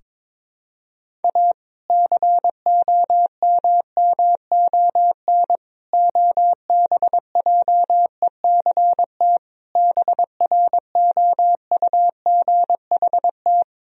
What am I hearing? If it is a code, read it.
Morse code, heard as a keyed tone, about 22 wpm, 700 Hz. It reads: A COMMON OBJECT BROUGHT
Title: Sentences from Top 1000 Words - Review Part 3 - 22wpm